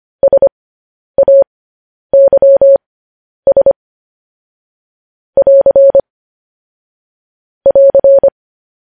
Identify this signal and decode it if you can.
Morse code, heard as a keyed tone, about 25 wpm, spaced as for 10 wpm, 550 Hz. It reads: SAYS <AR> <AR>
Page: news-headlines-in-morse-code-at-10-wpm